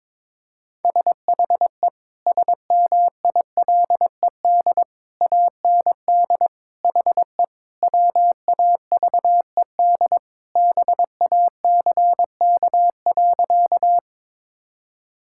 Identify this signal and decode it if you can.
Morse code, heard as a keyed tone, about 22 wpm, 700 Hz. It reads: SHE SMILED AND HE WAVED BACK.
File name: practice / 22wpm / Sentences with 2-Word Phrases - 22wpm